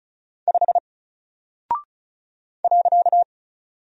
Morse code, heard as a keyed tone, about 36 wpm, 700 Hz. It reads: 5 .
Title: Single Character (Letter:Number:Punctuation:Prosign) 35wpm